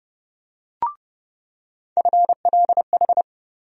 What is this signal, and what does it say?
Morse code, heard as a keyed tone, about 32 wpm, 700 Hz. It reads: FLH
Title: USA Call Sign Postfixes - 3 Letters 30wpm